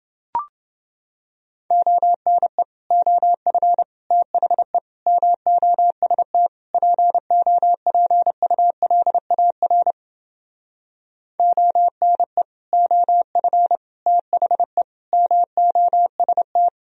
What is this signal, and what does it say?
Morse code, heard as a keyed tone, about 30 wpm, 700 Hz. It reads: ONE OF THE MOST POPULAR ONE OF THE MOST